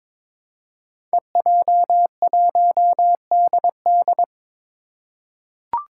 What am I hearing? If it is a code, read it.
Morse code, heard as a keyed tone, about 22 wpm, 700 Hz. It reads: EJ1DD
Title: Five Challenging Letter-Number Combinations 22wpm